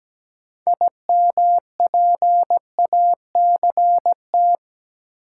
Morse code, heard as a keyed tone, about 17 wpm, 700 Hz. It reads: IMPACT
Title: Sentences from Top 1000 Words - Review Part 6 - New Word Drill - 17wpm